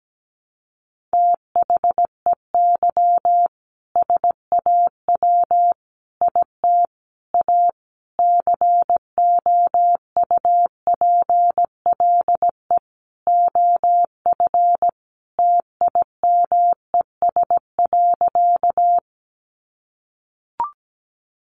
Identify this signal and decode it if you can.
Morse code, heard as a keyed tone, about 17 wpm, 700 Hz. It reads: THEY SAW IT A COUPLE OF TIMES.
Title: Sentences from Top 1100 Words 17wpm